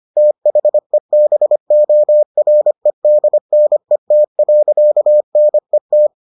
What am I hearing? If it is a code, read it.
Morse code, heard as a keyed tone, about 25 wpm, 600 Hz. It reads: THEBOREDNET.NET